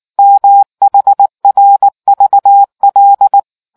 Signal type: Morse code